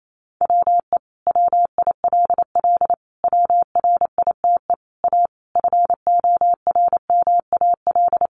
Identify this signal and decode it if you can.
Morse code, heard as a keyed tone, about 28 wpm, 700 Hz. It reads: WE WILL WRITE A FORMAL
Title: Sentences from Top 1300 Words 28wpm